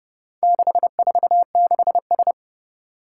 Morse code, heard as a keyed tone, about 32 wpm, 700 Hz. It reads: 646S
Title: Four Challenging Letter-Number Combinations 30wpm